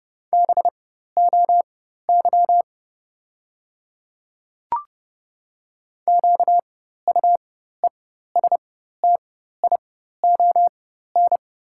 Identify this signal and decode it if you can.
Morse code, heard as a keyed tone, about 30 wpm, spaced as for 14 wpm, 700 Hz. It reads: BOY QUESTION
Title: 4x Character Spacing - Top 500 Words - 30wpm